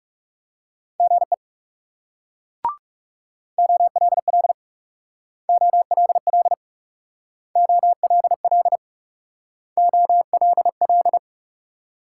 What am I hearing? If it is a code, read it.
Morse code, heard as a keyed tone, about 37 wpm, 700 Hz. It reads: GE OLL OLL OLL OLL
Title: Speed-Racing Multi-letter Phonemes - 30wpm